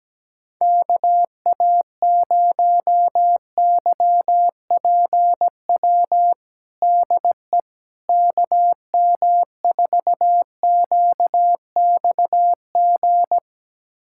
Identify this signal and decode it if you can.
Morse code, heard as a keyed tone, about 17 wpm, 700 Hz. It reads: KA0YPW DE KM4QXG
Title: USA Callsign DE Callsign - 6 Characters - 17wpm